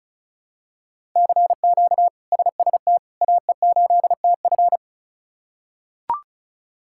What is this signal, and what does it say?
Morse code, heard as a keyed tone, about 36 wpm, 700 Hz. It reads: CQ SST AE8TF
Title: K1USN SST QSO Practice - 35wpm